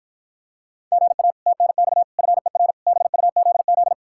Morse code, heard as a keyed tone, about 52 wpm, 700 Hz. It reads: GA TNX FER BUZZ